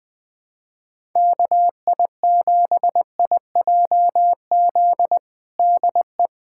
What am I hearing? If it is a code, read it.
Morse code, heard as a keyed tone, about 20 wpm, 700 Hz. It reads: KI7IJZ DE
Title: USA Callsign DE Callsign 20wpm